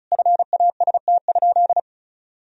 Morse code, heard as a keyed tone, about 34 wpm, 700 Hz. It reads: FAST?
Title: Sentences from Top 1000 Words - Review Part 2 - 35wpm